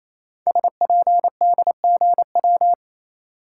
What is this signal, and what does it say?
Morse code, heard as a keyed tone, about 28 wpm, 700 Hz. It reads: SPDGW
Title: Code Groups - 5 Letters - 28wpm